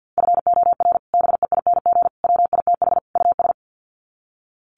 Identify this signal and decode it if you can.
Morse code, heard as a keyed tone, about 48 wpm, 700 Hz. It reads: FOR BEING WITH US
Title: Sets of 4 Words 50wpm